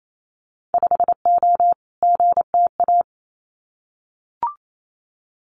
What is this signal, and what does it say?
Morse code, heard as a keyed tone, about 28 wpm, 690 Hz. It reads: 5O GTA
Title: Winter Field Day Exchanges Encore 28wpm